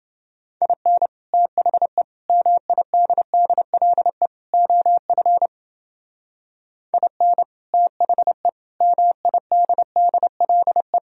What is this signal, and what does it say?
Morse code, heard as a keyed tone, about 29 wpm, 700 Hz. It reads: IN THE MIDDLE OF IN THE MIDDLE